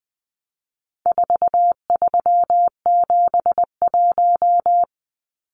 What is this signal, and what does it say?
Morse code, heard as a keyed tone, about 20 wpm, 700 Hz. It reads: 4371
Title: Numbers - 4-Digit 20wpm